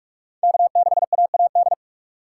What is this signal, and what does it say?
Morse code, heard as a keyed tone, about 44 wpm, 700 Hz. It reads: K 6 A A D
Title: USA Call Signs - 5 Characters - 6x Character Spacing - 45wpm